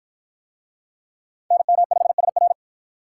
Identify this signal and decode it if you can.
Morse code, heard as a keyed tone, about 44 wpm, 700 Hz. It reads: NM5SR